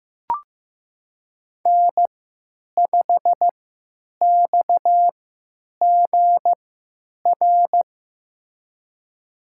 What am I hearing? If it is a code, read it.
Morse code, heard as a keyed tone, about 15 wpm, spaced as for 9 wpm, 700 Hz. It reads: N5XGR
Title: USA Call Signs - 3x Character Spacing - 15wpm